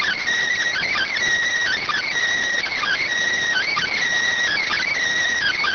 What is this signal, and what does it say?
Emergency Alert System (SAME) tones (header bursts), heard as harsh buzzing bursts. It is heard from 0.7 to 4.9 s.
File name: findings / CT 205 ax mips, iDose 3 0004.gen-32 rx